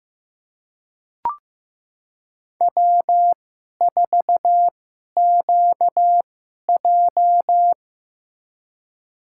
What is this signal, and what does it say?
Morse code, heard as a keyed tone, about 15 wpm, spaced as for 11 wpm, 700 Hz. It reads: W4QJ
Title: USA Call Signs - 2x Character Spacing - 15wpm